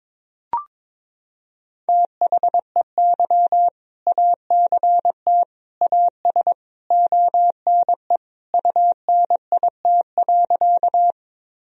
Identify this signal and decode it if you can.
Morse code, heard as a keyed tone, about 22 wpm, 700 Hz. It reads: THEY ACT AS ONE UNIT.
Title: Sentences from Top 1000 Words - Review Part 2 - 22wpm